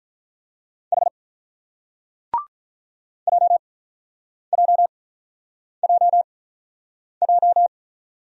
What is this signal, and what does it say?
Morse code, heard as a keyed tone, about 48 wpm, spaced as for 9 wpm, 700 Hz. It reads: H JJJJ